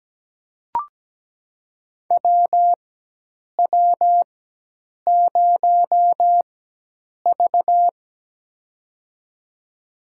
Morse code, heard as a keyed tone, about 17 wpm, spaced as for 8 wpm, 700 Hz. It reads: WW0V
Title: USA Call Signs - 4 Characters - 4x Character Spacing - 17wpm